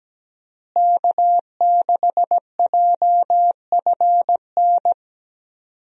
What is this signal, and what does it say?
Morse code, heard as a keyed tone, about 17 wpm, 700 Hz. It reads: K6JFN
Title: USA Call Signs - 5 Characters - Repeated 2x - 17wpm